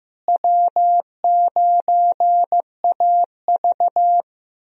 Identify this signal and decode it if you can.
Morse code, heard as a keyed tone, about 15 wpm, 700 Hz. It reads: W9AV
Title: USA Call Signs - 4 Characters - Repeated 2x - 15wpm